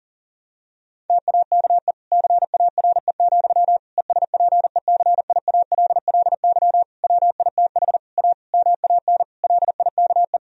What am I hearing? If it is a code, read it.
Morse code, heard as a keyed tone, about 40 wpm, 700 Hz. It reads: TAKE CARE, ESPECIALLY WITH A MAN LIKE
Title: Sentences from Top 1000 Words - Review Part 4 - 40wpm